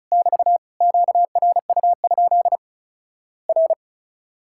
Morse code, heard as a keyed tone, about 36 wpm, 700 Hz. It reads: <BT> QRU? R
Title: Worked All Europe DX Contest - QTC Exchange - 35wpm